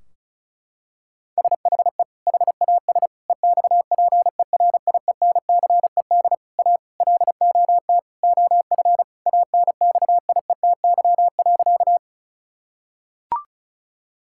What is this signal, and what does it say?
Morse code, heard as a keyed tone, about 36 wpm, 700 Hz. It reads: SHE HAS EXPERIENCED A LOT OF ANXIETY.
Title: Sentences from Top 1600 Words 35wpm